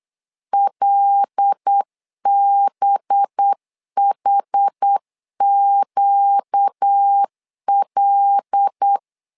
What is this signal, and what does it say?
Morse code, heard as a keyed tone, about 8 wpm, 800 Hz. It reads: LBHQL
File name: LEZIONI / 5 LBVC / 5 LB 8wpm 2816